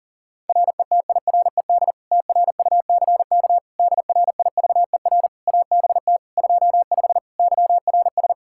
Morse code, heard as a keyed tone, about 40 wpm, 700 Hz. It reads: RETIRED TRUCK DRIVER ABT 25 YRS
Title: Practice QSO - Exchange 3 - More Info - 40wpm